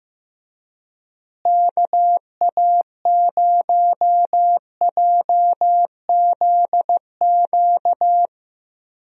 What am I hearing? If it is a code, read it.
Morse code, heard as a keyed tone, about 15 wpm, 700 Hz. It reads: KA0JZQ
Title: USA Call Signs - 6 Characters - Repeated 3x - 15wpm